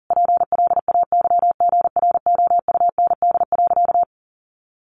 Morse code, heard as a keyed tone, about 37 wpm, 700 Hz. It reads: PLAYGROUND.
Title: Additional Easy Phrases 40wpm